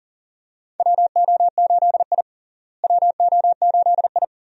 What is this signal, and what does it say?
Morse code, heard as a keyed tone, about 40 wpm, 700 Hz. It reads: WO8I WO8I
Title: USA Call Signs - 4 Characters - Repeated 3x - 40wpm